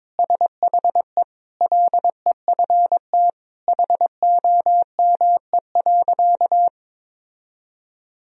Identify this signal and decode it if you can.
Morse code, heard as a keyed tone, about 22 wpm, 700 Hz. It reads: SHE LEFT HOME.
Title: Sentences from Top 1000 Words - Review Part 2 - 22wpm